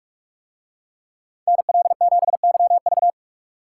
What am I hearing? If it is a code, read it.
Morse code, heard as a keyed tone, about 41 wpm, 700 Hz. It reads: NL7YV